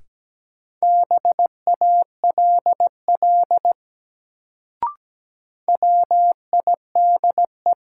Morse code, heard as a keyed tone, about 17 wpm, 700 Hz. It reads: BALL WIDE
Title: Sentences from Top 900 Words - New Word Drill - 17wpm